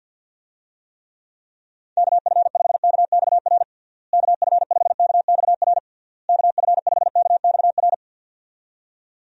Morse code, heard as a keyed tone, about 49 wpm, 700 Hz. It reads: KF5KXR KF5KXR KF5KXR
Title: USA Call Signs - 6 Characters - Repeated 3x - 50wpm